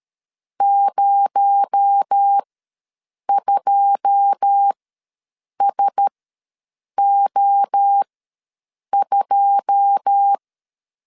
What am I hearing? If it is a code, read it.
Morse code, heard as a keyed tone, about 13 wpm, spaced as for 7 wpm, 800 Hz. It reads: 02SO2
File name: LEZIONI / 9 2 ws 12wpm 3406